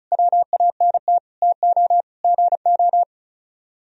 Morse code, heard as a keyed tone, about 42 wpm, 700 Hz. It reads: WANT TO GO
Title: Speed-Racing - Sets of 5 Words 35wpm